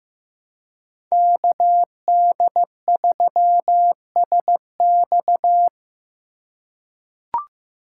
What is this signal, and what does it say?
Morse code, heard as a keyed tone, about 15 wpm, 700 Hz. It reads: KD3SX
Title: USA Call Signs - 5 Characters - 4x Character Spacing - 15wpm